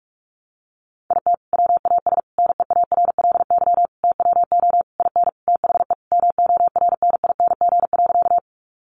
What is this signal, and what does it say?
Morse code, heard as a keyed tone, about 44 wpm, 700 Hz. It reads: IT WAS NEARLY TWO IN THE MORNING.
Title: Sentences from Top 1300 Words 45wpm